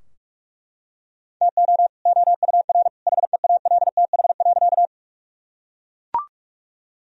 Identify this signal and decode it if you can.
Morse code, heard as a keyed tone, about 41 wpm, 700 Hz. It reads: TO OUR HEALTH.
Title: Sentences from Top 1000 Words - Review Part 3 - 45wpm